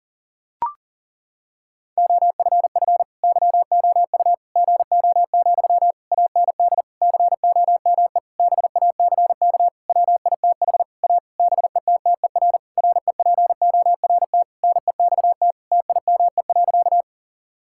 Morse code, heard as a keyed tone, about 40 wpm, 700 Hz. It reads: OFF YOU GO, AND COME BACK WITH A BETTER REPORT NEXT TIME.
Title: Sentences from Top 1000 Words - Review Part 6 - 40wpm